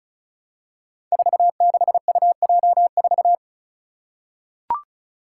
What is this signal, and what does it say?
Morse code, heard as a keyed tone, about 36 wpm, 700 Hz. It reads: 46UJ4
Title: Five Challenging Letter-Number Combinations 35wpm